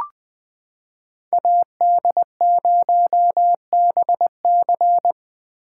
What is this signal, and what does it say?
Morse code, heard as a keyed tone, about 20 wpm, 700 Hz. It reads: AD0BC
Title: 2023 POTA Hunter Call Signs - 20wpm